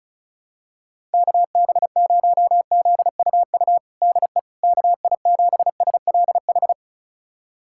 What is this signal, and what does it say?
Morse code, heard as a keyed tone, about 36 wpm, 700 Hz. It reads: KB0ZUU DE KI7SLH
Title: USA Callsign DE Callsign - 6 Characters - 35wpm